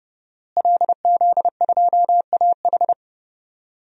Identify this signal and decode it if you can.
Morse code, heard as a keyed tone, about 30 wpm, 700 Hz. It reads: LZ2AH
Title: DXCC-100 Call Signs - 5 Characters - Repeated 2x - 30wpm